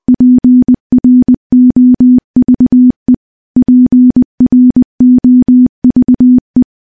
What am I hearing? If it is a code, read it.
Morse code, heard as a keyed tone, about 20 wpm, 250 Hz. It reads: PROVE PROVE
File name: morse-training-20WPM